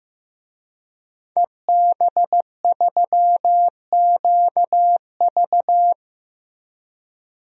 Morse code, heard as a keyed tone, about 15 wpm, 700 Hz. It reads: EB3QV
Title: DXCC-100 Call Signs - 5 Characters - Repeated 3x - 15wpm